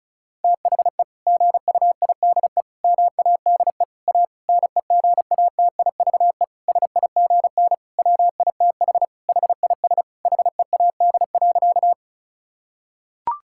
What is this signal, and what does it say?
Morse code, heard as a keyed tone, about 34 wpm, 700 Hz. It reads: THE GUIDE MADE A NEGATIVE SIGN WITH HIS HEAD.
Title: Sentences from Top 1000 Words - Review Part 6 - 35wpm